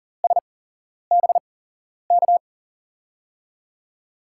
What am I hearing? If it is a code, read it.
Morse code, heard as a keyed tone, about 40 wpm, spaced as for 11 wpm, 700 Hz. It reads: SBK
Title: USA Call Signs - 6 Characters - 8x Character Spacing - 40wpm